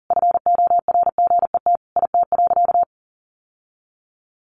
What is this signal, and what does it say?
Morse code, heard as a keyed tone, about 40 wpm, 700 Hz. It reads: FORGET IT.
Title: Sentences from Top 1200 Words 40wpm